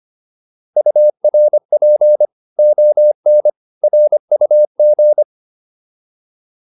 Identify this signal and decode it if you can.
Morse code, heard as a keyed tone, about 25 wpm, 600 Hz. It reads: URP ON RUG